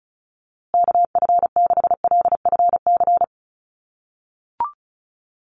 Morse code, heard as a keyed tone, about 34 wpm, 700 Hz. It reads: KF6LFC
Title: USA Call Signs Repeated 3x 35wpm